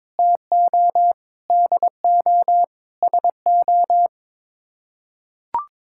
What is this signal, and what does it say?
Morse code, heard as a keyed tone, about 22 wpm, 700 Hz. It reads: TO DO SO
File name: practice / 22wpm / Sets of 6 Words - 22wpm